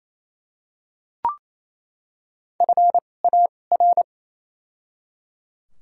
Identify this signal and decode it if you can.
Morse code, heard as a keyed tone, about 28 wpm, spaced as for 20 wpm, 700 Hz. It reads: FAR
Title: 2x Character Spacing - Top 500 Words - 28wpm